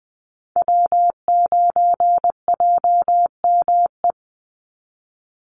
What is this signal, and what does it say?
Morse code, heard as a keyed tone, about 20 wpm, 700 Hz. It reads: W9JME
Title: USA Call Signs 20wpm